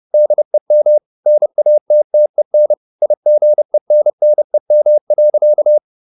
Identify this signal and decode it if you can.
Morse code, heard as a keyed tone, about 30 wpm, 600 Hz. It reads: DEM NATTEN IGENNEM.
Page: Ferie 30 wpm